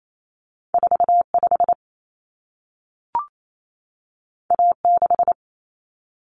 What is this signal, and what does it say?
Morse code, heard as a keyed tone, about 28 wpm, 700 Hz. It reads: 45 A6
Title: Common Cut Numbers - 2-Digit 28wpm